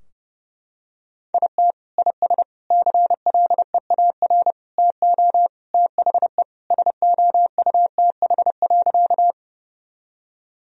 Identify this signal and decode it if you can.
Morse code, heard as a keyed tone, about 30 wpm, 700 Hz. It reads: IT IS CLEAR TO THE SOUTH.
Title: Sentences from Top 1000 Words - Review Part 2 - 30wpm